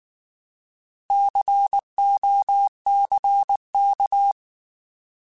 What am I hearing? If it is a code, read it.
Morse code, heard as a keyed tone, about 19 wpm, 800 Hz. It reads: COCK
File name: Alphabetical / Alphabetical mp3s / Cs @18